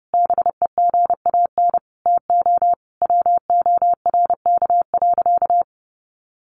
Morse code, heard as a keyed tone, about 30 wpm, 700 Hz. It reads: BEGAN TO WORK.
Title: Sentences from Top 1400 Words 30wpm